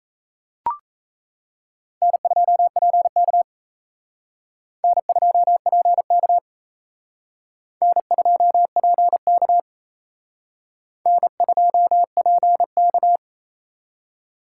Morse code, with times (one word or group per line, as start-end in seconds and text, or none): 2.0-3.4 N2PK
4.8-6.4 N2PK
7.8-9.6 N2PK
11.1-13.2 N2PK